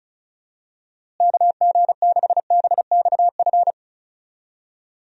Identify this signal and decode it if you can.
Morse code, heard as a keyed tone, about 34 wpm, 700 Hz. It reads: KG6BXF
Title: USA Call Signs - 6 Characters - Repeated 3x - 35wpm